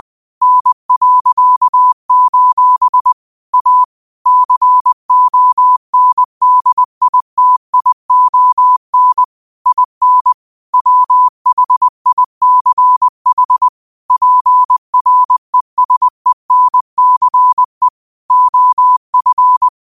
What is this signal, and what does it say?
Morse code, heard as a keyed tone, about 20 wpm, 1000 Hz. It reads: N.: A CONDITION IN WHICH PRESENCE OF
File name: qotdfiles / 20wpm